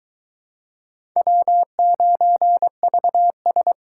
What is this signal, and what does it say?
Morse code, heard as a keyed tone, about 22 wpm, 700 Hz. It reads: W9VS